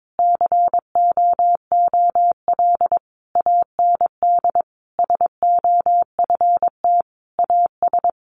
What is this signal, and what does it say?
Morse code, heard as a keyed tone, about 22 wpm, 700 Hz. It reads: COOL AND SOFT AS
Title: Sentences from Top 1500 Words 22wpm